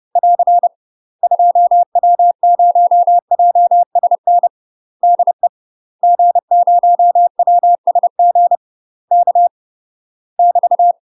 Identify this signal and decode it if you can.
Morse code, heard as a keyed tone, about 30 wpm, 700 Hz. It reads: <AR> 2W0JSN DE G0WSG K <BT>
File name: UK MASTER QSOs TRAINING / 30WPM / UK MASTER QSOs 0012